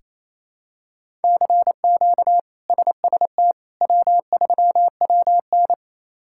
Morse code, heard as a keyed tone, about 28 wpm, 700 Hz. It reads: CQ SST W3WN
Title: K1USN SST QSO Practice - 28wpm